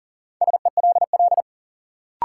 Morse code, heard as a keyed tone, about 40 wpm, 700 Hz. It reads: SELL
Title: Sentences from Top 1000 Words - Review Part 6 - New Word Drill - 40wpm